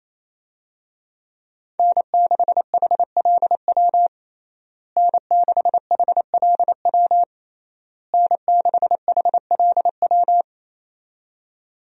Morse code, heard as a keyed tone, about 28 wpm, 700 Hz. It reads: N6HLW N6HLW N6HLW
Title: USA Call Signs - 5 Characters - Repeated 3x - 28wpm